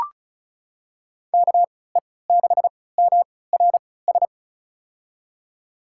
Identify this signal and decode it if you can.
Morse code, heard as a keyed tone, about 36 wpm, spaced as for 20 wpm, 700 Hz. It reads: KE6MRS
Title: USA Call Signs - 6 Characters - 3x Character Spacing - 35wpm